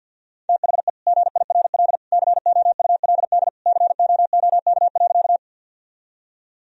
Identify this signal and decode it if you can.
Morse code, heard as a keyed tone, about 50 wpm, 700 Hz. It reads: THE GIRL COULD COOK.
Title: Sentences from Top 1000 Words - Review Part 4 - 50wpm